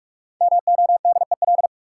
Morse code, heard as a keyed tone, about 46 wpm, 700 Hz. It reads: MODEL
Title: Sentences from Top 1000 Words - New Word Drill - 45wpm